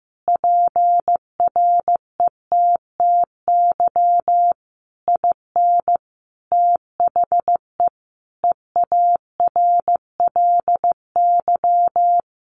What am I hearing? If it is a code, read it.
Morse code, heard as a keyed tone, about 15 wpm, 700 Hz. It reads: PRETTY IN THE EARLY
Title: Sentences from Top 1300 Words 15wpm